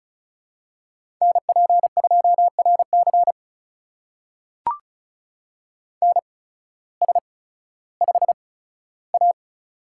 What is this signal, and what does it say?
Morse code, heard as a keyed tone, about 35 wpm, spaced as for 16 wpm, 700 Hz. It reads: NP2RC NS5A
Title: USA Call Signs - 8x Character Spacing - 35wpm